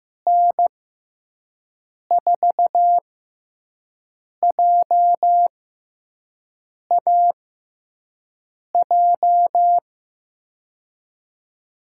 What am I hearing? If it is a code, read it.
Morse code, heard as a keyed tone, about 15 wpm, spaced as for 5 wpm, 700 Hz. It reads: N4JAJ
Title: USA Call Signs - 6x Character Spacing - 15wpm